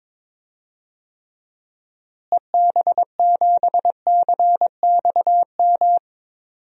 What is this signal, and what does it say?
Morse code, heard as a keyed tone, about 21 wpm, 700 Hz. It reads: EB7CXM
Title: DXCC-100 Call Signs - 6 Characters - Repeated 2x - 22wpm